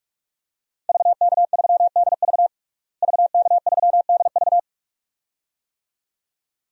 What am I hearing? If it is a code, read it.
Morse code, heard as a keyed tone, about 47 wpm, 700 Hz. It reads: VK3DV VK3DV
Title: DXCC-100 Call Signs - 5 Characters - Repeated 2x - 45wpm